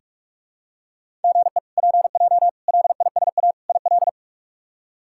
Morse code, heard as a keyed tone, about 46 wpm, 700 Hz. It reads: GE PJ LISA IL
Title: K1USN SST QSO Practice - 45wpm